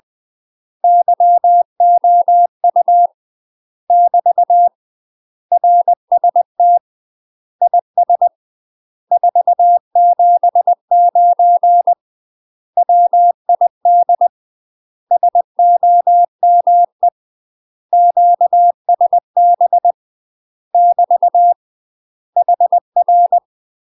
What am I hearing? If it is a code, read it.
Morse code, heard as a keyed tone, about 20 wpm, 700 Hz. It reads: YOU <BT> RST IS 479 WID SOME QSB <BT> HR
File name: UK MASTER QSOs TRAINING / 20WPM / UK MASTER QSOs 0051